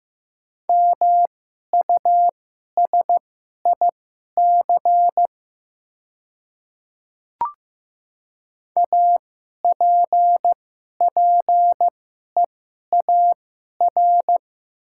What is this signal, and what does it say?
Morse code, heard as a keyed tone, about 15 wpm, spaced as for 11 wpm, 700 Hz. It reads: MUSIC APPEAR
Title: 2x Character Spacing - Top 500 Words - 15wpm